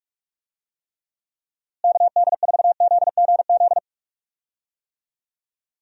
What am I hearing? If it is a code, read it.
Morse code, heard as a keyed tone, about 41 wpm, 700 Hz. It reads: KD4ZGZ
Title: USA Call Signs - 6 Characters - Repeated 3x - 45wpm